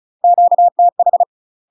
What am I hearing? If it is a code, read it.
Morse code, heard as a keyed tone, about 34 wpm, 700 Hz. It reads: QTH
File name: UK MASTER QSOs TRAINING / 35WPM / UK MASTER QSOs 0030